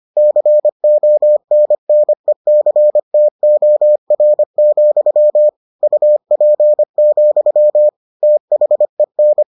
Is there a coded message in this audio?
Yes: Morse code from 0.2 to 9.4 s, heard as a keyed tone, about 25 wpm, 600 Hz. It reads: CONNECTOR, UP, THEN